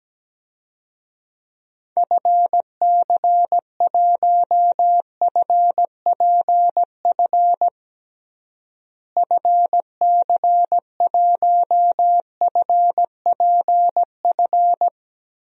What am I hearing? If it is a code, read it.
Morse code, heard as a keyed tone, about 17 wpm, 700 Hz. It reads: FC1FPF FC1FPF
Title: DXCC-100 Call Signs - 6 Characters - Repeated 3x - 17wpm